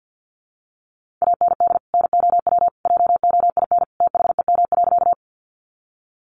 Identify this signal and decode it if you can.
Morse code, heard as a keyed tone, about 50 wpm, 700 Hz. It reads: AND NOW JOIN THEM.
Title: Sentences from Top 1400 Words 50wpm